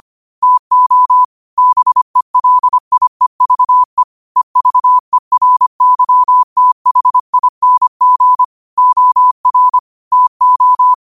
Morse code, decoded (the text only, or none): TO BELIEVE EVERYTHING OR TO